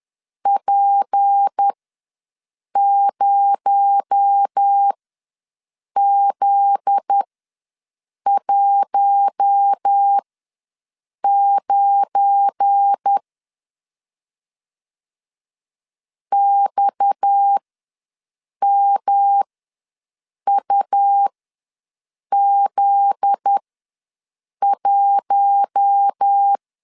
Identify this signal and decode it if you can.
Morse code, heard as a keyed tone, about 10 wpm, spaced as for 6 wpm, 800 Hz. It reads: P0Z19 XMUZ1
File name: LEZIONI / 7 GF9Z / 7 GF9Z ws 10wpm 3919